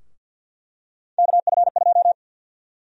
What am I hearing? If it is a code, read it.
Morse code, heard as a keyed tone, about 53 wpm, 700 Hz. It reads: KF2
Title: USA Call Sign Prefixes - Repeated 2x - 50wpm